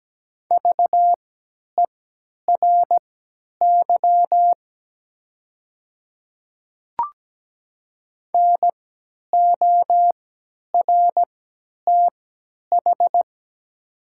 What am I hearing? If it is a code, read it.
Morse code, heard as a keyed tone, about 17 wpm, spaced as for 10 wpm, 700 Hz. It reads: VERY NORTH
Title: 3x Character Spacing - Top 300 Words - 17wpm